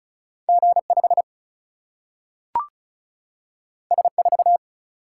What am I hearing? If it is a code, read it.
Morse code, heard as a keyed tone, about 36 wpm, 700 Hz. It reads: G5 S4
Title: Single-Letter Single-Number 35wpm